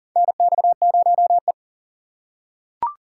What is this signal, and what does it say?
Morse code, heard as a keyed tone, about 40 wpm, 700 Hz. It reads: NX0E
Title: USA Call Signs - 4 Characters - 2x Character Spacing - 40wpm